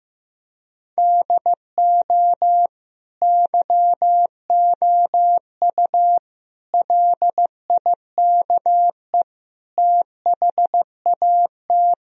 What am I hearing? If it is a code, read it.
Morse code, heard as a keyed tone, about 15 wpm, 700 Hz. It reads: DO YOU LIKE THAT
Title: Sentences from Top 1000 Words - Review Part 4 - 15wpm